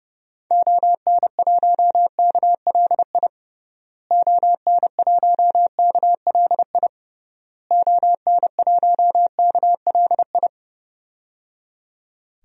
Morse code, heard as a keyed tone, about 29 wpm, 700 Hz. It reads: ON1KLI ON1KLI ON1KLI
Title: DXCC-100 Call Signs - 6 Characters - Repeated 3x - 30wpm